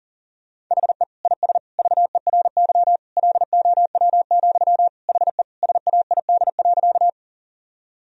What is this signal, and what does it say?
Morse code, heard as a keyed tone, about 37 wpm, 700 Hz. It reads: HE IS VERY LOW, HE SAID.
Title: Sentences from Top 1000 Words - Review Part 1 - 40wpm